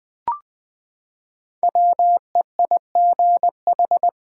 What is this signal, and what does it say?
Morse code, heard as a keyed tone, about 20 wpm, 700 Hz. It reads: WEIGH
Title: Five Letter Words - 20wpm